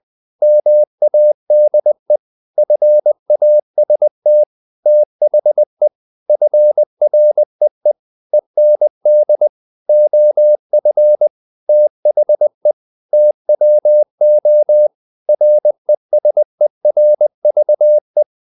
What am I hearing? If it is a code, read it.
Morse code, heard as a keyed tone, about 20 wpm, 600 Hz. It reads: MADE FAST THE FREE END OF THE TWO RESERVE
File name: the old man and the sea / the old man and the sea part 1 20wpm